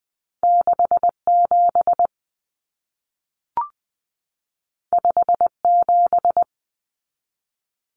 Morse code, heard as a keyed tone, about 20 wpm, 700 Hz. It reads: 67 57
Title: Common Cut Numbers - 2-Digit 20wpm